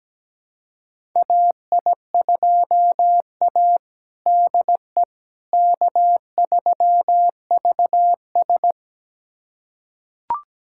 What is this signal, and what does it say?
Morse code, heard as a keyed tone, about 17 wpm, 700 Hz. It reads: AI2A DE K3VS
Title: USA Callsign DE Callsign - 4 Characters - 17wpm